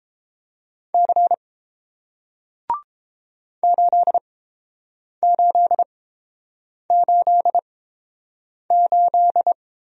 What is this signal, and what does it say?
Morse code, heard as a keyed tone, about 30 wpm, spaced as for 7 wpm, 700 Hz. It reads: C 8888